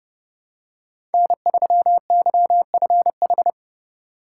Morse code, heard as a keyed tone, about 30 wpm, 700 Hz. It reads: N3YFH
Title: USA Call Signs - 5 Characters - 3x Character Spacing - 30wpm